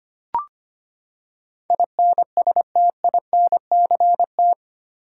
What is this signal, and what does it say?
Morse code, heard as a keyed tone, about 25 wpm, 700 Hz. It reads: INSTINCT
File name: Eight Letter Words - 25wpm